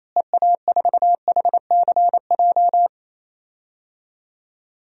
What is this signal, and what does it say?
Morse code, heard as a keyed tone, about 27 wpm, 700 Hz. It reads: EA4HCJ
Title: DXCC-100 Call Signs - 6 Characters - Repeated 2x - 28wpm